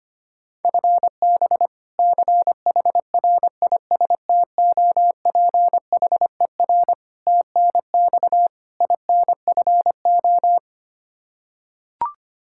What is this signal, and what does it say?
Morse code, heard as a keyed tone, about 25 wpm, 700 Hz. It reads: FB CHRISTOPHER TNX INFO
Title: K1USN SST QSO Practice 25wpm